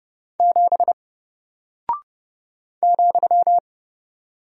Morse code, heard as a keyed tone, about 32 wpm, 700 Hz. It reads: 7 ,
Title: Single Character (Letter:Number:Punctuation:Prosign) Alt 30wpm